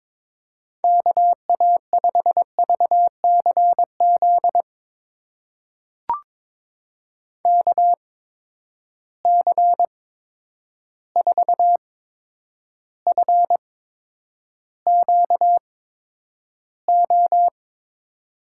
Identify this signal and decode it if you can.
Morse code, heard as a keyed tone, about 22 wpm, spaced as for 9 wpm, 700 Hz. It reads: KA5VCZ KC4FQO
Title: USA Call Signs - 6 Characters - 8x Character Spacing - 22wpm